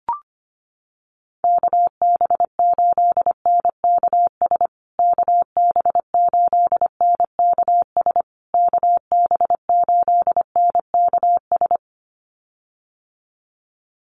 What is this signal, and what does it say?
Morse code, heard as a keyed tone, about 25 wpm, 700 Hz. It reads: KB8NKS KB8NKS KB8NKS
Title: USA Call Signs Repeated 3x 25wpm